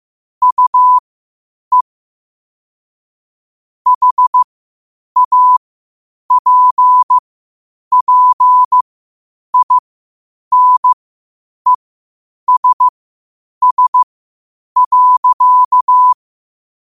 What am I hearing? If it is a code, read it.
Morse code, heard as a keyed tone, about 15 wpm, spaced as for 8 wpm, 1000 Hz. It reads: UE HAPPINESS.
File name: qotdfiles / 8wpm